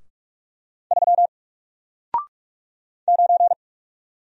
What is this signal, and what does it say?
Morse code, heard as a keyed tone, about 44 wpm, 700 Hz. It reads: 3 9
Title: Single Letter-Number Alt 45wpm